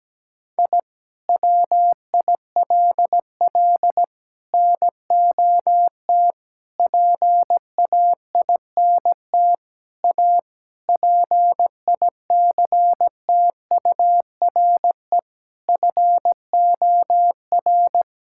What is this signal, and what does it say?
Morse code, heard as a keyed tone, about 17 wpm, 700 Hz. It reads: I WILL NOT PAINT A PICTURE FOR
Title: Sentences from Top 1000 Words - Review Part 6 - 17wpm